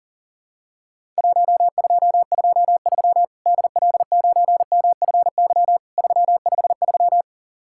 Morse code, heard as a keyed tone, about 40 wpm, 700 Hz. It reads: 1223 DL9MFY 353
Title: Worked All Europe DX Contest - QTC Exchange - 40wpm